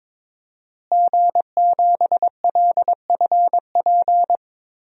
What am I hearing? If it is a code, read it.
Morse code, heard as a keyed tone, about 22 wpm, 700 Hz. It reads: G7LFP
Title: DXCC-100 Call Signs - 5 Characters - Repeated 3x - 22wpm